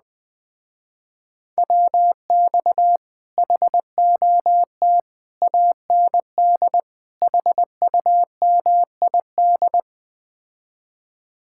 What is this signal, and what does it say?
Morse code, heard as a keyed tone, about 20 wpm, 700 Hz. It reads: WX HOT AND HUMID
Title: Speed-Racing Common QSO Phrases Encore - 17wpm